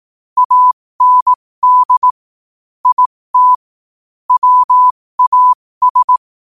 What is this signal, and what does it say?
Morse code, heard as a keyed tone, about 18 wpm, spaced as for 15 wpm, 1000 Hz. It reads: AND IT WAS
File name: qotdfiles / 15wpm